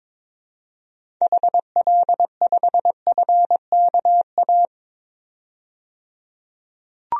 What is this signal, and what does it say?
Morse code, heard as a keyed tone, about 22 wpm, 700 Hz. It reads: HL5FKA
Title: DXCC-100 Call Signs - 6 Characters - 22wpm